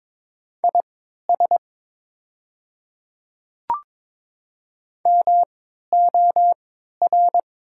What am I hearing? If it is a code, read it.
Morse code, heard as a keyed tone, about 22 wpm, spaced as for 12 wpm, 700 Hz. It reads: IS MOR
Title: 3x Character Spacing - Top 200 Words - 22wpm